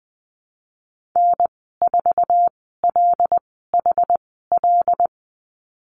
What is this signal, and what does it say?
Morse code, heard as a keyed tone, about 20 wpm, spaced as for 15 wpm, 700 Hz. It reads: N4LHL
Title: 076 Lesson 13g - Letter L - Call Signs 20wpm